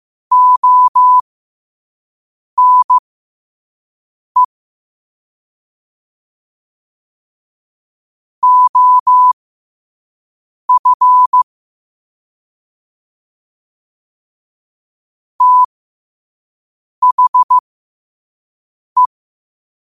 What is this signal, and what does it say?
Morse code, heard as a keyed tone, about 15 wpm, spaced as for 5 wpm, 1000 Hz. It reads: ONE OF THE
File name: qotdfiles / 5wpm